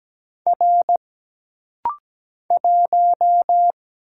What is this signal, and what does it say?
Morse code, heard as a keyed tone, about 17 wpm, 700 Hz. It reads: R 1
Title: Single Letter-Number Alt - Rapid-Fire 17wpm